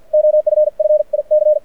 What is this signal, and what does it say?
Morse code, heard as a keyed tone, about 50 wpm, 600 Hz. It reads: OFF IC
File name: archived / CRSnet